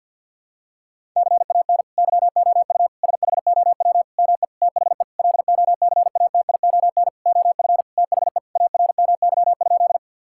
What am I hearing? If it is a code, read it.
Morse code, heard as a keyed tone, about 52 wpm, 700 Hz. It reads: CAN YOU SHOW ME THE LOCATION OF THE ARMY?
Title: Sentences from Top 1000 Words - Review Part 6 - 50wpm